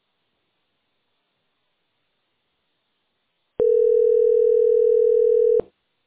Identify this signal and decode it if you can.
Ringback tone (North America), 440 + 480 Hz.